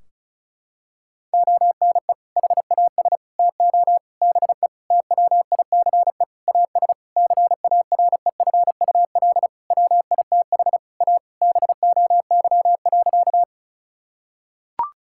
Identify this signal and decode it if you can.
Morse code, heard as a keyed tone, about 36 wpm, 700 Hz. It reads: ONE HAS TO BE TWICE AS CAREFUL WITH A BOY.
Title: Sentences from Top 1600 Words 35wpm